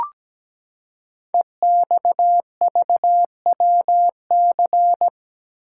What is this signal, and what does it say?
Morse code, heard as a keyed tone, about 17 wpm, 700 Hz. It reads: EXVWC
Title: Code Groups - 5 Letters - 17wpm